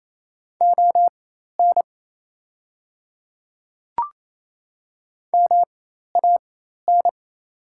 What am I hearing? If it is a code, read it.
Morse code, heard as a keyed tone, about 27 wpm, spaced as for 13 wpm, 690 Hz. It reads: ON MAN
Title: 4x Character Spacing - Top 200 Words - 28wpm